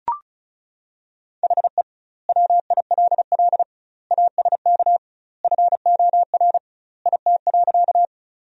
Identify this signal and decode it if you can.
Morse code, heard as a keyed tone, about 36 wpm, 700 Hz. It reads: HE WILL ASK FOR IT.
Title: 2x Word-Spacing - Sentences from Top 200 Words - 35wpm